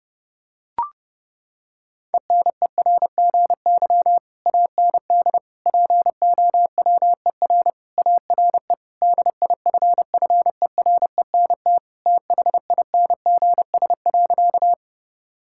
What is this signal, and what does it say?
Morse code, heard as a keyed tone, about 30 wpm, 700 Hz. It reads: ENERGY AND POWER ARE DIFFERENT THINGS.
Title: Sentences from Top 1000 Words - Review Part 6 - 30wpm